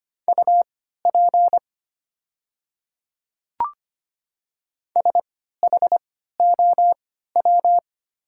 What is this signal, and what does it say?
Morse code, heard as a keyed tone, about 25 wpm, spaced as for 14 wpm, 690 Hz. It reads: UP SHOW
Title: 3x Character Spacing - Top 200 Words - 25wpm